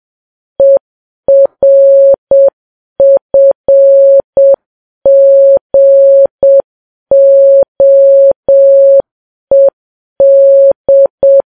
Morse code, heard as a keyed tone, about 7 wpm, 550 Hz. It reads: ERFGOED
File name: Files Web tls / Morse on9fv / morse7kps